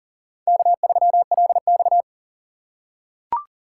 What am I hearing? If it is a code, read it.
Morse code, heard as a keyed tone, about 40 wpm, 700 Hz. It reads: K3LX
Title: USA Call Signs - 4 Characters - 3x Character Spacing - 40wpm